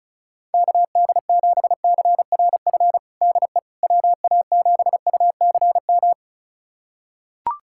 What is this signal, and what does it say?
Morse code, heard as a keyed tone, about 36 wpm, 700 Hz. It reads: KD7CRF DE WA7UCM
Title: USA Callsign DE Callsign - 6 Characters - 35wpm